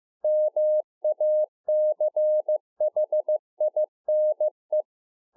Morse code, heard as a keyed tone, about 15 wpm, 610 Hz. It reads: MACHINE